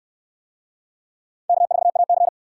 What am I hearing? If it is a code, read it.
Morse code, heard as a keyed tone, about 69 wpm, 700 Hz. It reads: K4AC